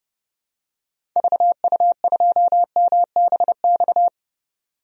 Morse code, heard as a keyed tone, about 32 wpm, 700 Hz. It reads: VU2MBX
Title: DXCC-100 Call Signs - 6 Characters - 30wpm